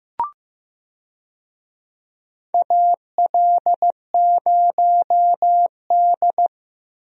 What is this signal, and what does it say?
Morse code, heard as a keyed tone, about 15 wpm, 700 Hz. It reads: AL0D
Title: USA Call Signs - 4 Characters - Repeated 2x - 15wpm